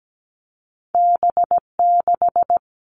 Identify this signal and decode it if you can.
Morse code, heard as a keyed tone, about 17 wpm, 700 Hz. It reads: B6
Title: All Cut Numbers - 2-Digit 17wpm